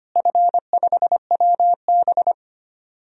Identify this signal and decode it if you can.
Morse code, heard as a keyed tone, about 25 wpm, 700 Hz. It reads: F5WB
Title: DXCC-100 Call Signs - 4 Characters - Repeated 2x - 25wpm